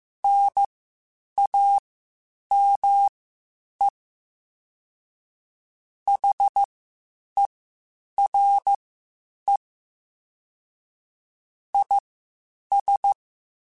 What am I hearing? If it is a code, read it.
Morse code, heard as a keyed tone, about 15 wpm, spaced as for 8 wpm, 800 Hz. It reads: NAME HERE IS
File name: (spaced) / QSOs Spaced-mp3s / WA7QOX 2 @13 sp